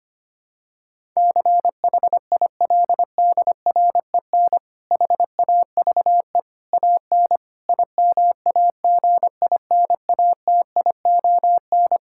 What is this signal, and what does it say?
Morse code, heard as a keyed tone, about 25 wpm, 700 Hz. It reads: CHILDREN HAVE AN IMAGINATION
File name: practice / 25wpm / Difficult Sentences with Hints - 25wpm